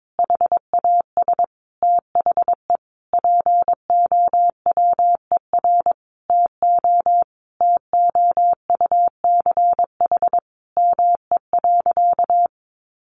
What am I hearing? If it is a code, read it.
Morse code, heard as a keyed tone, about 22 wpm, 700 Hz. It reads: HAS THE POWER TO TOUCH ME.
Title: Sentences from Top 1200 Words 22wpm